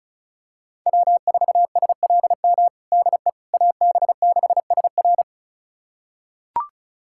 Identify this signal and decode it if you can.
Morse code, heard as a keyed tone, about 34 wpm, 700 Hz. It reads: W4SLM DE AB6SR
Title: USA Callsign DE Callsign - 5 Characters - 35wpm